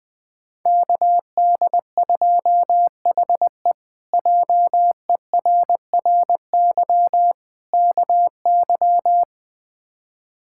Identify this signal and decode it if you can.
Morse code, heard as a keyed tone, about 21 wpm, 700 Hz. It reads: KD2HE JERRY KY
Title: K1USN SST QSO Practice - 20wpm